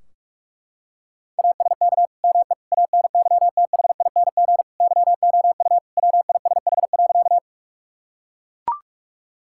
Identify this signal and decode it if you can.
Morse code, heard as a keyed tone, about 43 wpm, 700 Hz. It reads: ASK ME ANYTHING YOU WISH.
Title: Sentences from Top 1600 Words 45wpm